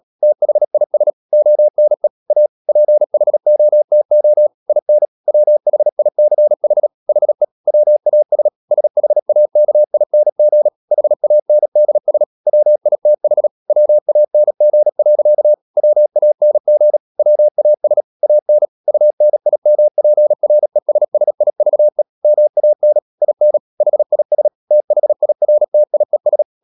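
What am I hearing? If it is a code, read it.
Morse code, heard as a keyed tone, about 37 wpm, 600 Hz. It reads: THIS ONE A PHOTO IN WHICH HE WAS SHAKING HANDS WITH WANG. WANG WAS AN UNIMPRESSIVE MAN IN HIS THIRTIES
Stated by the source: Morse code practice at 35–39 wpm sending English text of Ken Follett's novel Never, from CW med Gnister.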